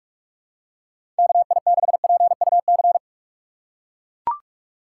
Morse code, heard as a keyed tone, about 41 wpm, 700 Hz. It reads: KI6PUC
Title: USA Call Signs - 6 Characters - 2x Character Spacing - 45wpm